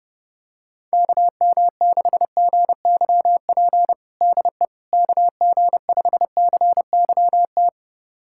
Morse code, heard as a keyed tone, about 30 wpm, 700 Hz. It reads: KM6GYP DE KG5CYT
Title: USA Callsign DE Callsign - 6 Characters - 30wpm